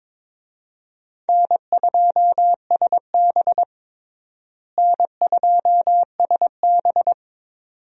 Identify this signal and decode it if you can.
Morse code, heard as a keyed tone, about 22 wpm, 700 Hz. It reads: N2SB N2SB
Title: USA Call Signs - 4 Characters - Repeated 2x - 22wpm